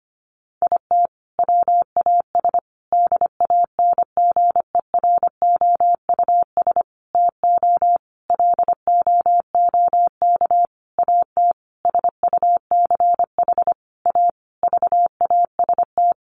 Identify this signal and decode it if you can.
Morse code, heard as a keyed tone, about 25 wpm, 700 Hz. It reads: IT WAS DANGEROUS TO LOOK AT SUCH A VAST
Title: Sentences from Top 1300 Words 25wpm